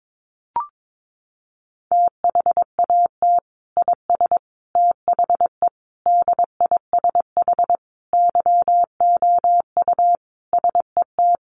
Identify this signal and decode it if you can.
Morse code, heard as a keyed tone, about 22 wpm, 700 Hz. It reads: THAT IS THE DISH YOU SET
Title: Sentences from Top 1300 Words 22wpm